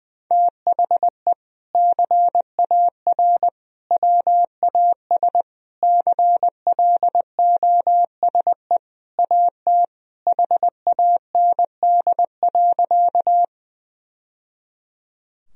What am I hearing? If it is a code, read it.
Morse code, heard as a keyed tone, about 21 wpm, 700 Hz. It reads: THE CAR WAS CLOSE AT HAND.
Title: Sentences from Top 1000 Words - Review Part 1 - 20wpm